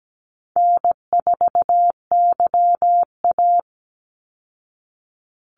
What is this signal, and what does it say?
Morse code, heard as a keyed tone, about 17 wpm, 700 Hz. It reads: N4YA
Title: Call Signs Repeated 3x - 17wpm